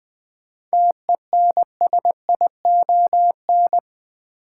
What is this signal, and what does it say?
Morse code, heard as a keyed tone, about 20 wpm, 700 Hz. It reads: TENSION
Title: Seven Letter Words - 20wpm